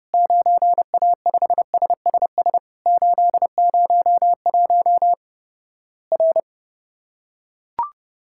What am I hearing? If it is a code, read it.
Morse code, heard as a keyed tone, about 32 wpm, 700 Hz. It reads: 9A5SSS 801 R
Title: Worked All Europe DX Contest - QTC Exchange - 30wpm